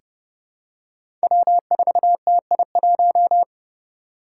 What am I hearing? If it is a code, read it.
Morse code, heard as a keyed tone, about 29 wpm, 700 Hz. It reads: W4TI1
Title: Five Challenging Letter-Number Combinations 30wpm